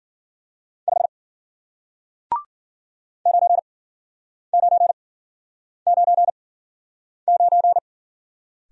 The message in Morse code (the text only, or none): 5 9999